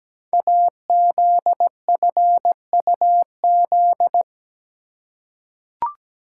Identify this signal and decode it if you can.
Morse code, heard as a keyed tone, about 17 wpm, 700 Hz. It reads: AZFUZ
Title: Code Groups - 5 Letters - 17wpm